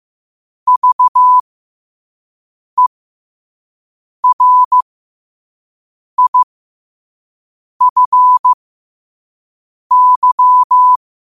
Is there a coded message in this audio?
Yes: Morse code from 0.7 to 11.0 s, heard as a keyed tone, about 15 wpm, spaced as for 5 wpm, 1000 Hz. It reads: VERIFY